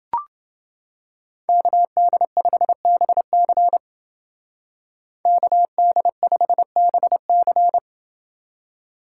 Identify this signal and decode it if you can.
Morse code, heard as a keyed tone, about 29 wpm, 700 Hz. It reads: KD5BC KD5BC